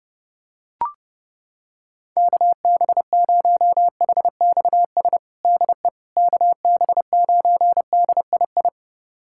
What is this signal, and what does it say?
Morse code, heard as a keyed tone, about 30 wpm, 700 Hz. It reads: KB0HXS DE KB9DII
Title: USA Callsign DE Callsign - 6 Characters - 30wpm